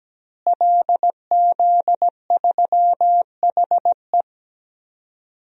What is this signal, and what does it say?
Morse code, heard as a keyed tone, about 17 wpm, 700 Hz. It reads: LZ3HE
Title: DXCC-100 Call Signs - 5 Characters - Repeated 2x - 17wpm